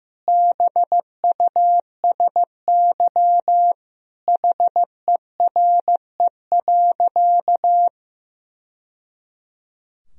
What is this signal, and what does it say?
Morse code, heard as a keyed tone, about 16 wpm, 700 Hz. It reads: BUSY HERE.
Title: Sentences from Top 1000 Words - Review Part 3 - 15wpm